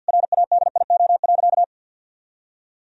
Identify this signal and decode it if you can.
Morse code, heard as a keyed tone, about 50 wpm, 700 Hz. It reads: RADIO.
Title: Sentences from Top 1000 Words - Review Part 3 - 50wpm